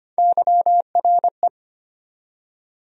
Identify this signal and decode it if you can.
Morse code, heard as a keyed tone, about 25 wpm, 700 Hz. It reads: YRE
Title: USA Call Sign Postfixes - 3 Letters 25wpm